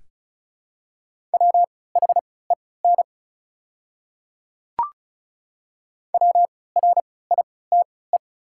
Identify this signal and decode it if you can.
Morse code, heard as a keyed tone, about 34 wpm, spaced as for 20 wpm, 700 Hz. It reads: WHEN WRITE
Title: 3x Character Spacing - Top 100 Words - 35wpm